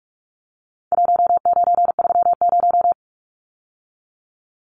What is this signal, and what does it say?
Morse code, heard as a keyed tone, about 44 wpm, 700 Hz. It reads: 1930
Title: Numbers - Contest Style - 4-Digit 45wpm